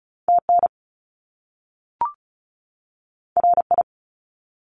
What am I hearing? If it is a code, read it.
Morse code, heard as a keyed tone, about 36 wpm, 700 Hz. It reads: TN RI
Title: US States 35wpm